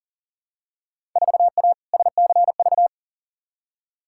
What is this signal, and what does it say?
Morse code, heard as a keyed tone, about 39 wpm, 700 Hz. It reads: 4A SCV
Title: Field Day Exchanges - 40wpm